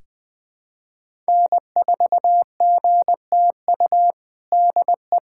Morse code, heard as a keyed tone, about 21 wpm, 700 Hz. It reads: N4GTU DE
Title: USA Callsign DE Callsign 20wpm